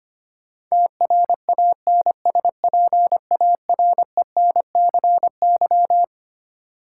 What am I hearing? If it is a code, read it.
Morse code, heard as a keyed tone, about 25 wpm, 700 Hz. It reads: TRANSPARENCY
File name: Twelve Letter Words - 25wpm